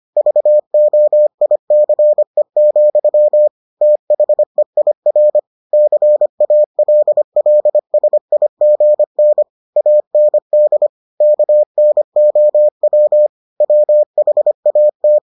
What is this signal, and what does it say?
Morse code, heard as a keyed tone, about 25 wpm, 600 Hz. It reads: VOICE, THEIR CALLSIGN AND KNOW WHAT